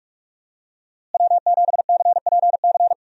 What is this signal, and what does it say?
Morse code, heard as a keyed tone, about 46 wpm, 700 Hz. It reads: W7CPC
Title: USA Call Signs - 5 Characters - Repeated 2x - 45wpm